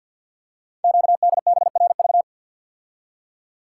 Morse code, heard as a keyed tone, about 50 wpm, 700 Hz. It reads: QDBRV
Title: Code Groups - 5 Letters - 50wpm